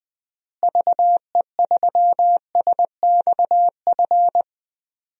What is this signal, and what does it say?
Morse code, heard as a keyed tone, about 20 wpm, 700 Hz. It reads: VE3SXF
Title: DXCC-100 Call Signs - 6 Characters - Repeated 2x - 20wpm